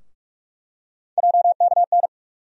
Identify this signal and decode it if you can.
Morse code, heard as a keyed tone, about 44 wpm, 700 Hz. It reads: JKN
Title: USA Call Sign Postfixes - 3 Letters 45wpm